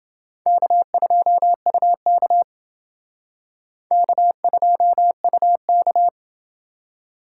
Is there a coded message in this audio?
Yes: Morse code from 0.5 to 6.1 s, heard as a keyed tone, about 28 wpm, 700 Hz. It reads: K2UK K2UK